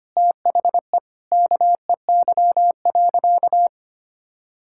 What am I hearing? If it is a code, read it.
Morse code, heard as a keyed tone, about 25 wpm, 700 Hz. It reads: THE KEY.
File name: practice / 25wpm / Sentences with 4-Word Phrases - 25wpm